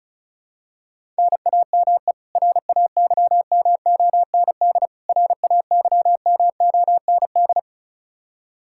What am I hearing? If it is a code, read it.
Morse code, heard as a keyed tone, about 36 wpm, 700 Hz. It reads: NAME RAYMOND RAYMOND
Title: Practice QSO - Exchange 1 - Beginning - 35wpm